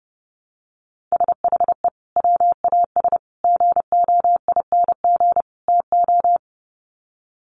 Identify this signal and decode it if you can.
Morse code, heard as a keyed tone, about 30 wpm, 700 Hz. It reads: SHE WAS GOING TO
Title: Sets of 4 Words 30wpm